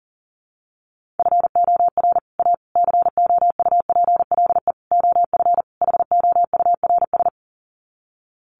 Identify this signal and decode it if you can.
Morse code, heard as a keyed tone, about 37 wpm, 700 Hz. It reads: FOR A COUPLE OF HOURS
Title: Sets of 5 Words Encore2 40wpm